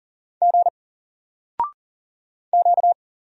Morse code, heard as a keyed tone, about 40 wpm, 700 Hz. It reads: G Q
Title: Single Character (Letter:Number:Punctuation:Prosign) Alt 40wpm